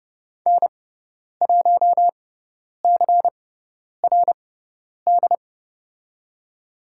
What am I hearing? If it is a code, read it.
Morse code, heard as a keyed tone, about 30 wpm, spaced as for 10 wpm, 700 Hz. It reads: N1CRD
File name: Lesson 16g - Letter c - Call Signs 30wpm